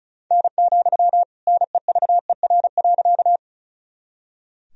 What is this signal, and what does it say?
Morse code, heard as a keyed tone, about 36 wpm, 700 Hz. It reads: N, NEVER.
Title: Sentences from Top 300 Words and Abbreviations 35wpm